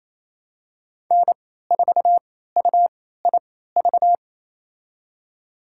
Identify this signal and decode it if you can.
Morse code, heard as a keyed tone, about 28 wpm, spaced as for 16 wpm, 700 Hz. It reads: N4UIV
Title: USA Call Signs - 3x Character Spacing - 28wpm